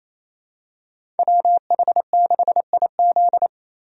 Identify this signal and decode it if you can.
Morse code, heard as a keyed tone, about 28 wpm, 700 Hz. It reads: WH6IZ
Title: USA Call Signs - 5 Characters - Repeated 2x - 28wpm